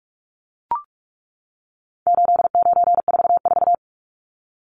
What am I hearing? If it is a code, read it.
Morse code, heard as a keyed tone, about 44 wpm, 700 Hz. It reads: 8944
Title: Numbers - 4-Digit 45wpm